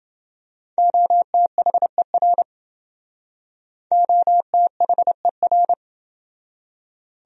Morse code, heard as a keyed tone, about 28 wpm, 700 Hz. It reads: OTHER OTHER